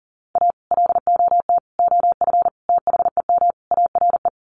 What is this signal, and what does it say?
Morse code, heard as a keyed tone, about 40 wpm, 700 Hz. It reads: A LOT OF THEM ARE
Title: Sets of 5 Words Encore2 40wpm